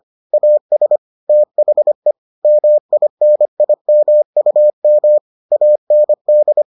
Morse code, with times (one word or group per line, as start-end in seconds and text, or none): 0.3-1.0 AS
1.3-2.1 THE
2.4-5.2 MINIMUM
5.5-6.6 AND